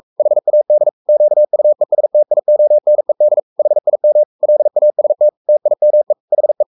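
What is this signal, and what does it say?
Morse code, heard as a keyed tone, about 44 wpm, 600 Hz. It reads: HAD QUESTIONED HIM LAST TIME HE
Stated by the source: Morse code at 43 wpm sending Ken Follett's novel Never in English, from CW med Gnister.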